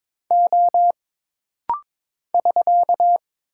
Morse code, heard as a keyed tone, about 22 wpm, 700 Hz. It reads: O <SK>
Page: This rapid-fire timing is geared to push your ICR (Instant Character Recognition) skills.